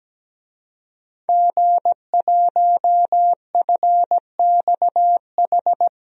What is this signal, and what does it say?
Morse code, heard as a keyed tone, about 17 wpm, 700 Hz. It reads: G1FXH
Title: DXCC-100 Call Signs - 5 Characters - 17wpm